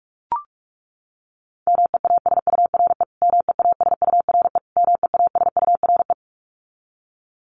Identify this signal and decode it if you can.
Morse code, heard as a keyed tone, about 44 wpm, 700 Hz. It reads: MEASURE MEASURE MEASURE
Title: Top 500 Words Repeated 45wpm